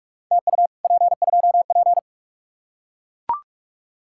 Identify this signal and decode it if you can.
Morse code, heard as a keyed tone, about 44 wpm, 700 Hz. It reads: TU P2P